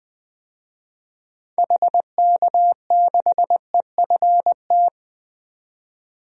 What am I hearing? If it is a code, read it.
Morse code, heard as a keyed tone, about 20 wpm, 700 Hz. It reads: HK6EFT
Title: DXCC-100 Call Signs - 6 Characters - 20wpm